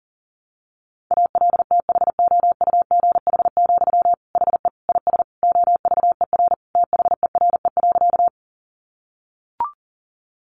Morse code, heard as a keyed tone, about 37 wpm, 700 Hz. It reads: ALTHOUGH, HE IS OVER THERE.
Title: Sentences from Top 600 Words 40wpm